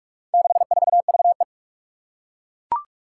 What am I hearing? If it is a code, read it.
Morse code, heard as a keyed tone, about 41 wpm, 700 Hz. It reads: 64VE
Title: Four Challenging Letter-Number Combinations 45wpm